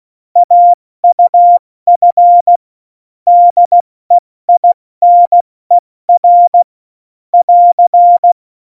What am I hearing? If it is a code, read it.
Morse code, heard as a keyed tone, about 16 wpm, spaced as for 14 wpm, 700 Hz. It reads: AUF DEINER <AR>
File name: Morsekurs / Witze / Witz 26. Januar 2021, 70 F80